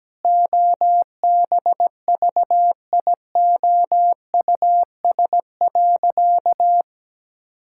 Morse code, heard as a keyed tone, about 17 wpm, 700 Hz. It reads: OBVIOUS.
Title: Sentences from Top 1600 Words 17wpm